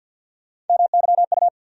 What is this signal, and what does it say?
Morse code, heard as a keyed tone, about 48 wpm, 700 Hz. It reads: MYU
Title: USA Call Sign Postfixes - 3 Letters 50wpm